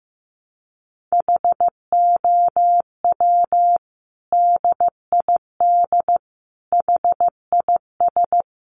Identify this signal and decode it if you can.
Morse code, heard as a keyed tone, about 15 wpm, 700 Hz. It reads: HOW DID HIS
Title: Sentences from Top 900 Words 15wpm